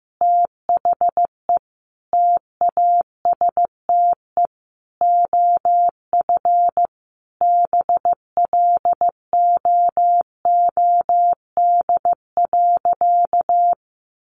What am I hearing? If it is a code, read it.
Morse code, heard as a keyed tone, about 15 wpm, 700 Hz. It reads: THE TASTE OF BLOOD.
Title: Sentences from Top 1200 Words 15wpm